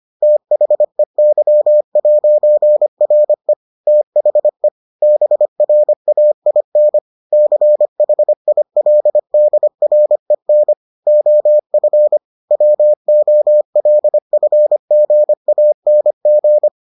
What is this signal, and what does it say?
Morse code, heard as a keyed tone, about 25 wpm, 600 Hz. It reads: THEY'RE THE BRAIN CHILDREN OF WOLFGANG